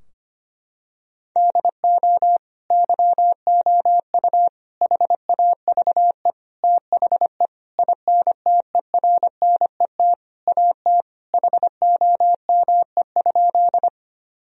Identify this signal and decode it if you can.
Morse code, heard as a keyed tone, about 25 wpm, 700 Hz. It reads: DO YOU HAVE THE INTERNET AT HOME?
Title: Sentences from Top 1000 Words - Review Part 4 - 25wpm